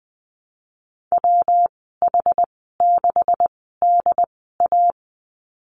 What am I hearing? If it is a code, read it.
Morse code, heard as a keyed tone, about 20 wpm, spaced as for 14 wpm, 700 Hz. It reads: WH6DA
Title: 140 Lesson 23g - Letter f - Call Signs 20wpm